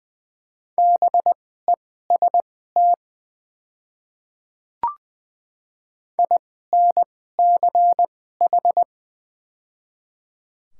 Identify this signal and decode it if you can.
Morse code, heard as a keyed tone, about 21 wpm, spaced as for 15 wpm, 700 Hz. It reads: BEST INCH
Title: 2x Character Spacing - Top 500 Words - 20wpm